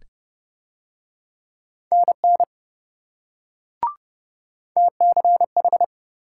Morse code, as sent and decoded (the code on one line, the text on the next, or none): -. -. / - -.-. ....
NN TCH